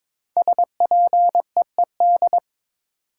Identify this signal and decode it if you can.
Morse code, heard as a keyed tone, about 22 wpm, 700 Hz. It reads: SPEED
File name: 5 Letter Words - First 3 Letters Given - 22wpm